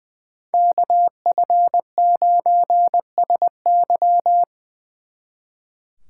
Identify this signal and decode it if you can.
Morse code, heard as a keyed tone, about 24 wpm, 700 Hz. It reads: KF9SY
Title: Speed Racing - USA 2x2 Call Signs 20wpm